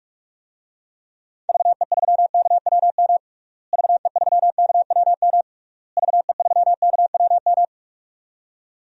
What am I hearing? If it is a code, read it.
Morse code, heard as a keyed tone, about 42 wpm, 700 Hz. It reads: VE3KWM VE3KWM VE3KWM
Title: DXCC-100 Call Signs - 6 Characters - Repeated 3x - 45wpm